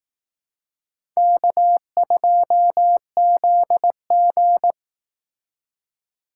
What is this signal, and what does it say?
Morse code, heard as a keyed tone, about 18 wpm, 700 Hz. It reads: K2ZG
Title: Speed Racing - USA 1x2 Call Signs 15wpm